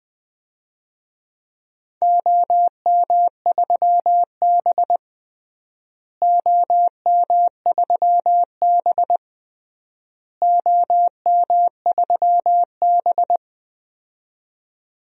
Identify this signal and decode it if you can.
Morse code, heard as a keyed tone, about 20 wpm, 700 Hz. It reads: OM3B OM3B OM3B
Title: DXCC-100 Call Signs - 4 Characters - Repeated 3x - 20wpm